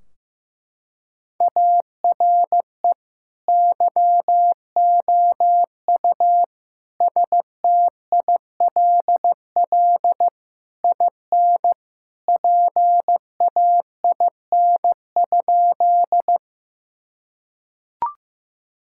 Morse code, heard as a keyed tone, about 14 wpm, 700 Hz. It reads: ARE YOU STILL IN PAIN?
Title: Sentences from Top 1600 Words 15wpm